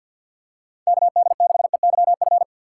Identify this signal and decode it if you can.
Morse code, heard as a keyed tone, about 48 wpm, 700 Hz. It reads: KD6EYF
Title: USA Call Signs - 6x Character Spacing - 50wpm